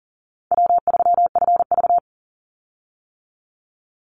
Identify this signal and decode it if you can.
Morse code, heard as a keyed tone, about 38 wpm, 700 Hz. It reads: W3FV
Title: Easy Call Signs - 40wpm